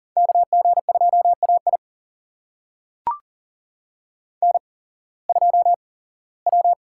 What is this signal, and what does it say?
Morse code, heard as a keyed tone, about 40 wpm, spaced as for 21 wpm, 700 Hz. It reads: KG2AI N2W
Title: USA Call Signs - 5 Characters - 8x Character Spacing - 40wpm